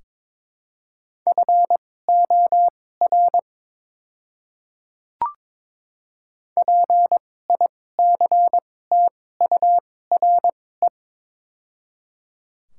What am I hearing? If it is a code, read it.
Morse code, heard as a keyed tone, about 23 wpm, spaced as for 16 wpm, 700 Hz. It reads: FOR PICTURE
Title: 2x Character Spacing - Top 500 Words - 22wpm